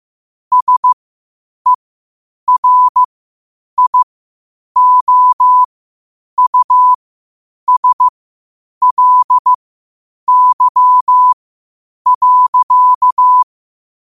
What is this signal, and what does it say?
Morse code, heard as a keyed tone, about 15 wpm, spaced as for 8 wpm, 1000 Hz. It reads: SERIOUSLY.
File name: qotdfiles / 8wpm